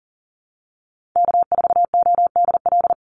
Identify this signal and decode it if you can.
Morse code, heard as a keyed tone, about 40 wpm, 700 Hz. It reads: K4ODL
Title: Call Signs - 40wpm